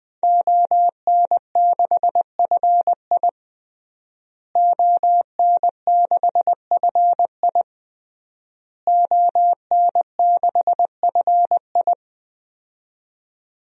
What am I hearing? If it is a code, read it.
Morse code, heard as a keyed tone, about 21 wpm, 700 Hz. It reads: ON6FI ON6FI ON6FI
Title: DXCC-100 Call Signs - 5 Characters - Repeated 3x - 20wpm